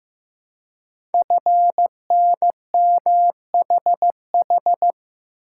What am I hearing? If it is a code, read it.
Morse code, heard as a keyed tone, about 15 wpm, 700 Hz. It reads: FNMHH
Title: Code Groups - 5 Letters - 15wpm